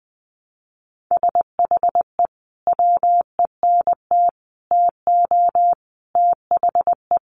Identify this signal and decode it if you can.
Morse code, heard as a keyed tone, about 20 wpm, 700 Hz. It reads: SHE WENT TO THE
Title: Sentences from Top 900 Words 20wpm